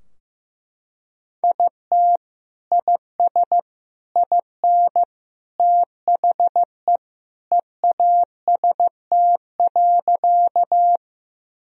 Morse code, heard as a keyed tone, about 15 wpm, 700 Hz. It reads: IT IS IN THE EAST.
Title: Sentences from Top 1000 Words - Review Part 3 - 15wpm